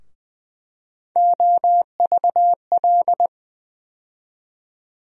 Morse code, heard as a keyed tone, about 20 wpm, 700 Hz. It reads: OVL
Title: USA Call Sign Postfixes - 3 Letters 20wpm